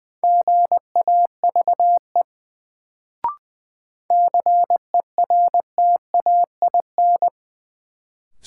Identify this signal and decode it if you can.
Morse code, heard as a keyed tone, about 20 wpm, 700 Hz. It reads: GAVE CERTAIN
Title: Sentences from Top 500 Words - New Word Drill - 20wpm